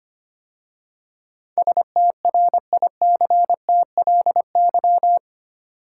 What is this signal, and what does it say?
Morse code, heard as a keyed tone, about 25 wpm, 690 Hz. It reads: STRICTLY
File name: 8 Letter Words - First 3 Letters Given - 25wpm